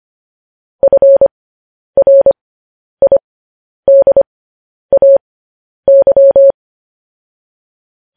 Morse code, heard as a keyed tone, about 25 wpm, spaced as for 10 wpm, 550 Hz. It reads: FRIDAY
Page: news headlines in morse code at 10 wpm